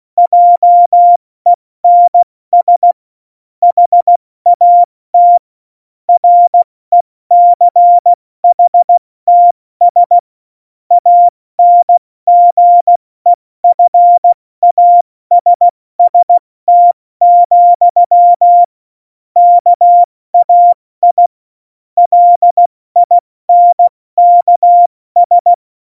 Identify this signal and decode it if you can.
Morse code, heard as a keyed tone, about 16 wpm, 700 Hz. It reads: JENS HAT RECHTS ANGEFASST, KAI LINKS
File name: Morsekurs / Witze / Witz 30. April 2019, 70 F80